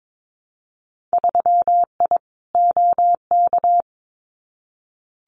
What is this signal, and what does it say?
Morse code, heard as a keyed tone, about 22 wpm, 700 Hz. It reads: 3I OK
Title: Winter Field Day Exchanges Encore 22wpm